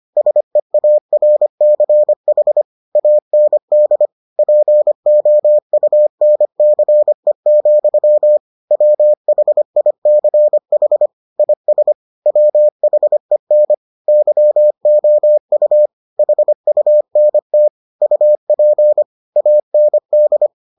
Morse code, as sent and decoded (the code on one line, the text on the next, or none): ... .- .-. -.-. .... / .- -. -.. / .--. --- ..- -. -.-. --..-- / .-- .... .. -.-. .... / .. ... / .-- .... -. / -.-- --- ..- / .... ..- -. - / ..- .--. / .- -. -..
SEARCH AND POUNCE, WHICH IS WHEN YOU HUNT UP AND